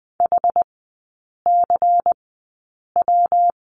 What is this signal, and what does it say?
Morse code, heard as a keyed tone, about 20 wpm, spaced as for 8 wpm, 700 Hz. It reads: HCW
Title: 114 Lesson 19e - Letter w - Three Characters 20wpm